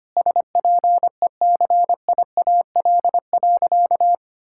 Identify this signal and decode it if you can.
Morse code, heard as a keyed tone, about 25 wpm, 700 Hz. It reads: SPECIAL.
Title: Sentences from Top 1600 Words 25wpm